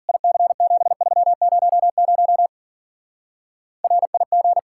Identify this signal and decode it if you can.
Morse code, heard as a keyed tone, about 47 wpm, 700 Hz. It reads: IC7300 RIG
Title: Speed-Racing Common QSO Phrases - 40wpm